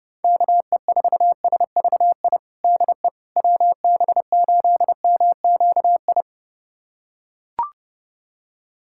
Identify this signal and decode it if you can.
Morse code, heard as a keyed tone, about 32 wpm, 700 Hz. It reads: KE4SVI DE WB8MQI
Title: USA Callsign DE Callsign - 6 Characters - 30wpm